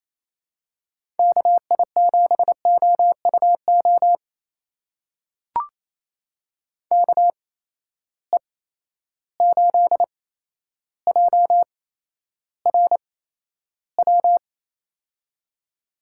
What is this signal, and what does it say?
Morse code, heard as a keyed tone, about 28 wpm, spaced as for 12 wpm, 700 Hz. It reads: KI7OUO KE8JRW
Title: USA Call Signs - 6 Characters - 8x Character Spacing - 28wpm